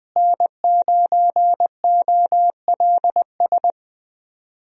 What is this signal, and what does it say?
Morse code, heard as a keyed tone, about 20 wpm, 700 Hz. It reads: N9OLS
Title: USA Call Signs - 5 Characters - Repeated 2x - 20wpm